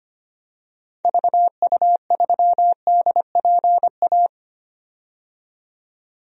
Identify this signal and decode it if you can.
Morse code, heard as a keyed tone, about 25 wpm, 700 Hz. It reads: VU3DPA
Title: DXCC-100 Call Signs - 6 Characters - 25wpm